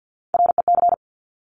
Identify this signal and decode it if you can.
Morse code, heard as a keyed tone, about 50 wpm, 700 Hz. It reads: REC
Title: Top Three Letter Combinations in Words 50wpm